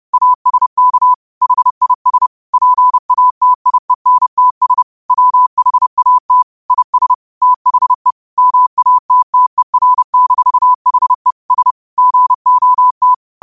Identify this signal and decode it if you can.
Morse code, heard as a keyed tone, about 30 wpm, 1000 Hz. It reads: ASK HIS PATIENTS WHAT IS THE MATTER-HE S GOT
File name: qotdfiles / 30wpm